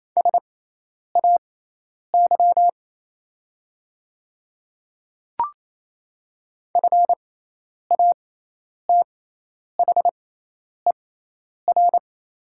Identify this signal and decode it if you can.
Morse code, heard as a keyed tone, about 28 wpm, spaced as for 10 wpm, 700 Hz. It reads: SAY FATHER
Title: 6x Character Spacing - Top 200 Words - 28wpm